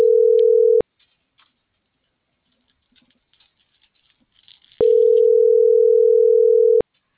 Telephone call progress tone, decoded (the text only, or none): ringback tone